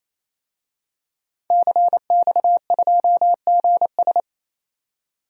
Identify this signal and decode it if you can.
Morse code, heard as a keyed tone, about 28 wpm, 700 Hz. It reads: CX2GS
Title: DXCC-100 Call Signs - 5 Characters - 28wpm